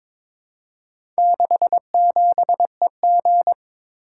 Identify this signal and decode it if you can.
Morse code, heard as a keyed tone, about 23 wpm, 700 Hz. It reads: 67EG
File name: practice / 22wpm / Four Letter-Number Combinations - 22wpm